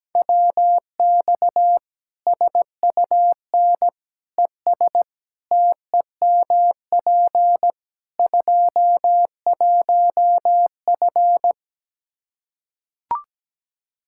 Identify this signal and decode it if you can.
Morse code, heard as a keyed tone, about 17 wpm, 700 Hz. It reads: WX SUN ES TEMP 21F
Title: Practice QSO - Exchange 2 - Info - 17wpm